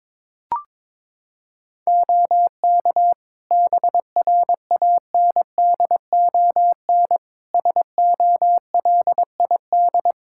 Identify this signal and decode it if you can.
Morse code, heard as a keyed tone, about 22 wpm, 700 Hz. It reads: OK BRANDON SOLID
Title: K1USN SST QSO Practice 22wpm